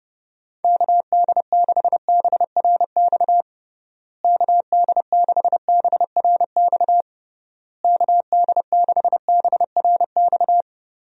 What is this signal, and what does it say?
Morse code, heard as a keyed tone, about 30 wpm, 700 Hz. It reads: KD6BRX KD6BRX KD6BRX
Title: USA Call Signs - 6 Characters - Repeated 3x - 30wpm